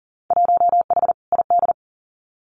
Morse code, heard as a keyed tone, about 40 wpm, 700 Hz. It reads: 1H ID
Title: Winter Field Day Exchanges 40wpm